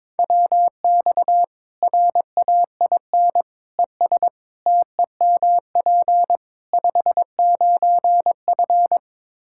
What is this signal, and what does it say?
Morse code, heard as a keyed tone, about 22 wpm, 700 Hz. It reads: WX RAIN ES TEMP 59F
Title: Practice QSO - Exchange 2 - Info - 22wpm